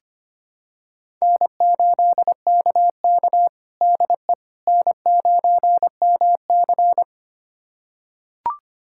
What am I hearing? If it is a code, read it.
Morse code, heard as a keyed tone, about 25 wpm, 700 Hz. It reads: N8KK DE N9MC
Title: USA Callsign DE Callsign - 4 Characters - 25wpm